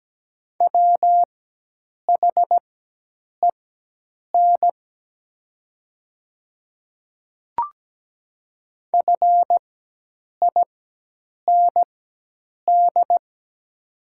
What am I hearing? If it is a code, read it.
Morse code, heard as a keyed tone, about 17 wpm, spaced as for 8 wpm, 700 Hz. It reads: WHEN FIND
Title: 4x Character Spacing - Top 100 Words - 17wpm